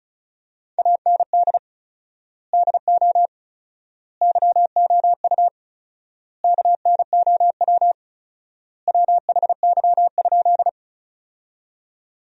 Morse code, heard as a keyed tone, about 34 wpm, 700 Hz. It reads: AND DO YOU KNOW WHY?
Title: 4x Word-Spacing - Sentences from Top 200 Words - 35wpm